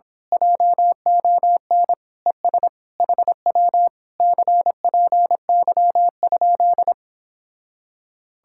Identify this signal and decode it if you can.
Morse code, heard as a keyed tone, about 26 wpm, 700 Hz. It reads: JON ES HW CPY?
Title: Speed-Racing Common QSO Phrases - 22wpm